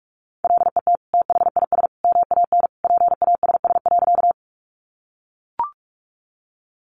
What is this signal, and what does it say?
Morse code, heard as a keyed tone, about 44 wpm, 700 Hz. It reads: LET THIS MAN PASS.
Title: Sentences from Top 400 Words 45wpm